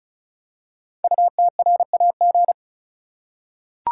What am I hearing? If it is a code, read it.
Morse code, heard as a keyed tone, about 36 wpm, 700 Hz. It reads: UTRAG
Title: Code Groups - 5 Letter:Numbers - 35wpm